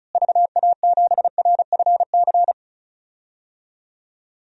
Morse code, heard as a keyed tone, about 36 wpm, 700 Hz. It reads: VA7RFC
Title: DXCC-100 Call Signs - 6 Characters - Repeated 3x - 35wpm